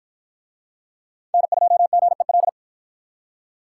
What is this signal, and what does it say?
Morse code, heard as a keyed tone, about 53 wpm, 700 Hz. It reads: N2GEL